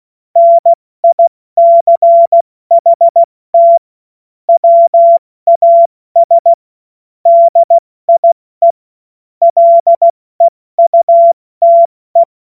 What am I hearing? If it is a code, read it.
Morse code, heard as a keyed tone, about 16 wpm, 680 Hz. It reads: NICHT WAS DIE LEUTE
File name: Witz-30-September-2025-70-F80